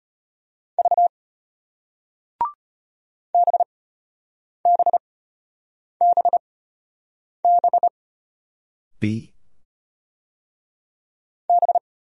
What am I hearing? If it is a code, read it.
Morse code, heard as a keyed tone, about 34 wpm, spaced as for 8 wpm, 700 Hz. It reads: V BBBB B